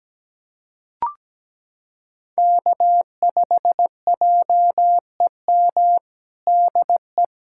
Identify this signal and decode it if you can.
Morse code, heard as a keyed tone, about 17 wpm, 700 Hz. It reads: K5JEM DE
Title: USA Callsign DE Callsign 17wpm